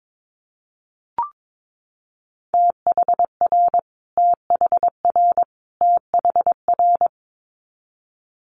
Morse code, heard as a keyed tone, about 22 wpm, 700 Hz. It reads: THR THR THR
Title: Multi-letter Phonemes Repeated 3x 22wpm